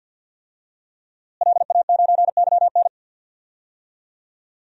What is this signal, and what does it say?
Morse code, heard as a keyed tone, about 46 wpm, 700 Hz. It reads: LA9YN
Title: DXCC-100 Call Signs - 5 Characters - Repeated 2x - 50wpm